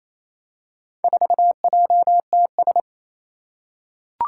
Morse code, heard as a keyed tone, about 28 wpm, 700 Hz. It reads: 4JTS
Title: Four Challenging Letter-Number Combinations 28wpm